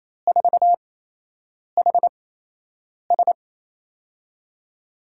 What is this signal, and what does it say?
Morse code, heard as a keyed tone, about 28 wpm, spaced as for 8 wpm, 700 Hz. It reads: 4HS
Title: USA Call Signs - 4 Characters - 8x Character Spacing - 28wpm